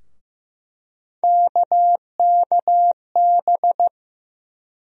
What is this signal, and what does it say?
Morse code, heard as a keyed tone, about 15 wpm, 700 Hz. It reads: KKB
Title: USA Call Sign Postfixes - 3 Letters 15wpm